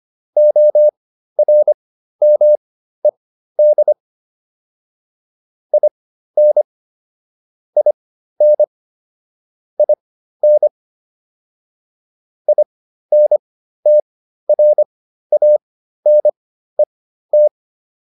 Morse code, heard as a keyed tone, about 25 wpm, spaced as for 13 wpm, 600 Hz. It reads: ORMED IN IN IN INTRANET